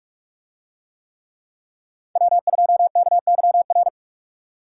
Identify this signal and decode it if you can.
Morse code, heard as a keyed tone, about 43 wpm, 700 Hz. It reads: W2KYR